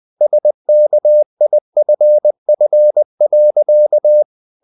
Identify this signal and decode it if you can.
Morse code, heard as a keyed tone, about 20 wpm, 600 Hz. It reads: SKIFF.
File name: the old man and the sea part 2 20wpm